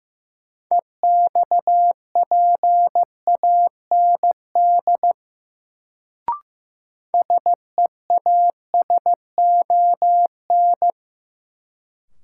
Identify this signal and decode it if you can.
Morse code, heard as a keyed tone, about 15 wpm, 700 Hz. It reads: EXPAND SEASON
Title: Sentences from Top 1400 Words - New Word Drill Alt - 15wpm